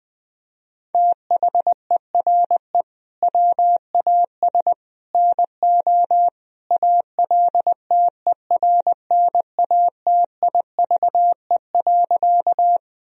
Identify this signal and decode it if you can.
Morse code, heard as a keyed tone, about 21 wpm, 700 Hz. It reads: THERE WAS NO ALTERNATIVE.
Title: Sentences from Top 1000 Words - Review Part 4 - 20wpm